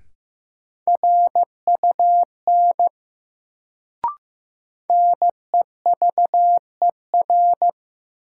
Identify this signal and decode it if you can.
Morse code, heard as a keyed tone, about 16 wpm, 700 Hz. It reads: RUN NEVER
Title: Sentences from Top 300 Words - New Word Drill - 15wpm